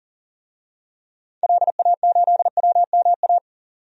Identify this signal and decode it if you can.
Morse code, heard as a keyed tone, about 39 wpm, 700 Hz. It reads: LA8WMA
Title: DXCC-100 Call Signs - 6 Characters - 40wpm